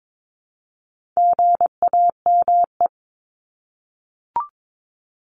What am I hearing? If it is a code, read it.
Morse code, heard as a keyed tone, about 22 wpm, 700 Hz. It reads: GAME
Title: Top 500 Words 22wpm